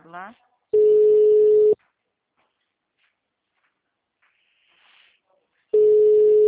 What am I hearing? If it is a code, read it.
Ringback tone (Europe, 425 Hz), with speech.
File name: out- -2016- - - .4